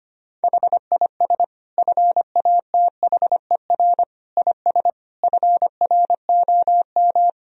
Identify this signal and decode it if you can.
Morse code, heard as a keyed tone, about 25 wpm, 700 Hz. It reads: HIS FATHER IS FROM
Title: Sentences from Top 1000 Words - Review Part 2 - 25wpm